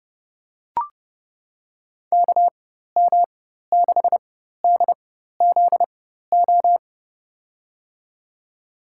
Morse code, heard as a keyed tone, about 30 wpm, spaced as for 14 wpm, 700 Hz. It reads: KM6DZO
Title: USA Call Signs - 6 Characters - 4x Character Spacing - 30wpm